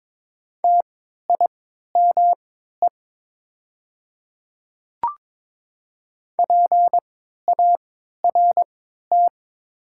Morse code, heard as a keyed tone, about 23 wpm, spaced as for 13 wpm, 690 Hz. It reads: TIME PART
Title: 3x Character Spacing - Top 500 Words - 22wpm